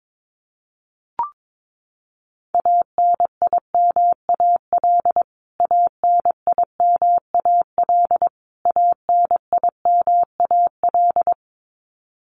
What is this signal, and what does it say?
Morse code, heard as a keyed tone, about 22 wpm, 700 Hz. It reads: ANIMAL ANIMAL ANIMAL
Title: Top 200 Words Repeated 22wpm